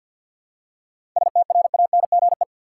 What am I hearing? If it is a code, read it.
Morse code, heard as a keyed tone, about 47 wpm, 700 Hz. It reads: STRANGE
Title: Seven Letter Words - 50wpm